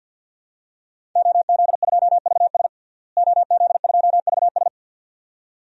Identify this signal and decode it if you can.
Morse code, heard as a keyed tone, about 48 wpm, 700 Hz. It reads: OZ2VS OZ2VS
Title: DXCC-100 Call Signs - 5 Characters - Repeated 2x - 50wpm